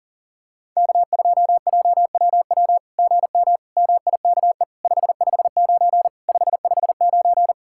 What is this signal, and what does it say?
Morse code, heard as a keyed tone, about 37 wpm, 690 Hz. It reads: K2JWW GM MIKE 559 559
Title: SOTA QSO Practice 40wpm